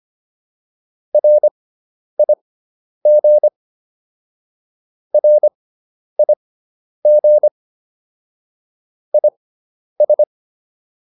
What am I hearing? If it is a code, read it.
Morse code, heard as a keyed tone, about 25 wpm, spaced as for 10 wpm, 600 Hz. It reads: RIG RIG IS